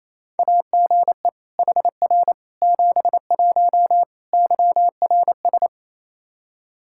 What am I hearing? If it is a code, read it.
Morse code, heard as a keyed tone, about 28 wpm, 700 Hz. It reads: AGE HR 71 YRS
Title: Practice QSO - Exchange 3 - More Info - 28wpm